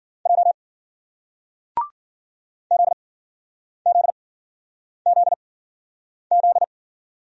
Morse code, heard as a keyed tone, about 54 wpm, spaced as for 9 wpm, 700 Hz. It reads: J ZZZZ